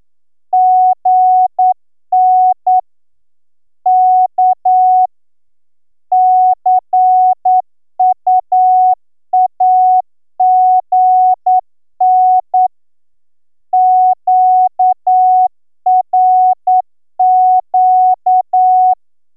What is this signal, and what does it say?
Morse code, heard as a keyed tone, about 10 wpm, 740 Hz. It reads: GN K CUAGN QRQ